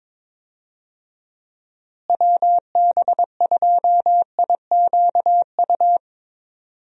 Morse code, heard as a keyed tone, about 22 wpm, 700 Hz. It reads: WB2IQU
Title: USA Call Signs - 6 Characters - Repeated 3x - 22wpm